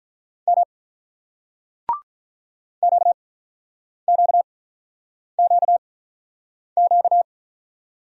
Morse code, heard as a keyed tone, about 48 wpm, spaced as for 9 wpm, 700 Hz. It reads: M QQQQ